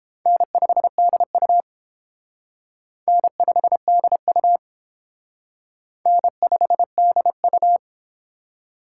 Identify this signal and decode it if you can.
Morse code, heard as a keyed tone, about 29 wpm, 700 Hz. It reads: N5DU N5DU N5DU